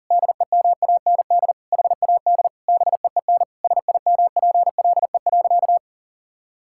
Morse code, heard as a keyed tone, about 37 wpm, 700 Hz. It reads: DEMAND HAD BEEN SIMPLE.
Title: Sentences from Top 1000 Words - Review Part 6 - 40wpm